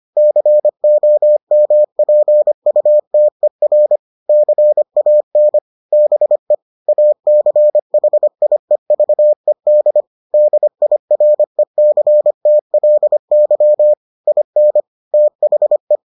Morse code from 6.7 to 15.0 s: ACHIEVED DIRECTLY IN

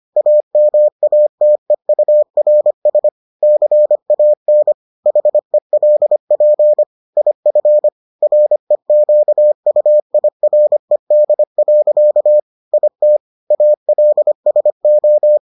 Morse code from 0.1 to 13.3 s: AMATEURS CAN HELP IF REQUIRED. IT

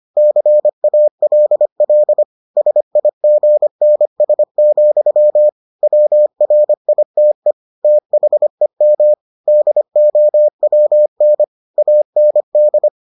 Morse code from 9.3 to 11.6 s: DOWN

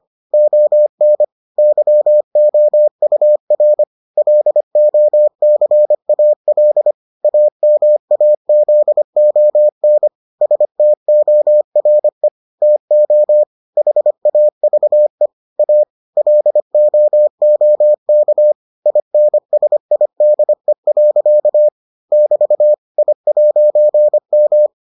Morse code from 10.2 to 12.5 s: STORE

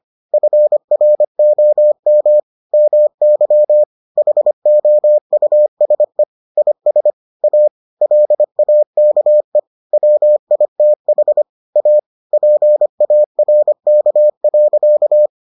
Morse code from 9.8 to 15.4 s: WITH A PARK.

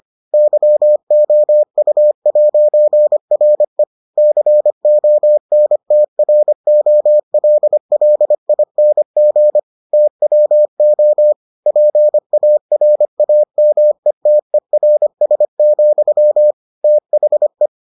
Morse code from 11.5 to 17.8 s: PARAMETERS, THE

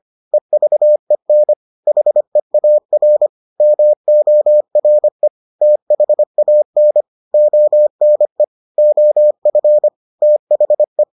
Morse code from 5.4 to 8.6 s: THAN ONE

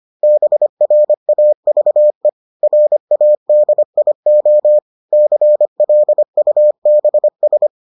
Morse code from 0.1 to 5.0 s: BRAVE RADIO

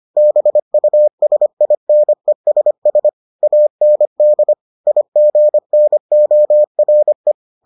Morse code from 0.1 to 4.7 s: BUSINESS AND

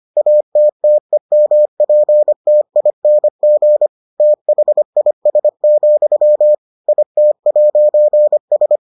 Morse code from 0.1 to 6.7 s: ATTEMPTING THIS,

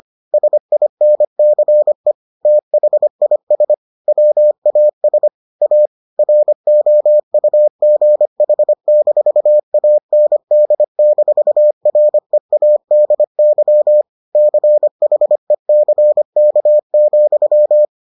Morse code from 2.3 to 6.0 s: THIS WAS A

